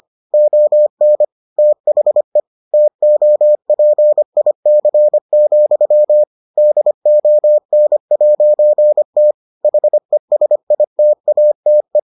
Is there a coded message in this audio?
Yes: Morse code from 0.3 to 12.0 s, heard as a keyed tone, about 25 wpm, 600 Hz. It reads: ON THE TOPIC, DON'T HESITATE